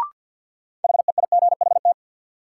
Morse code, heard as a keyed tone, about 47 wpm, 700 Hz. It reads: HEIGHT
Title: Sentences from Top 1600 Words - New Word Drill Alt - 50wpm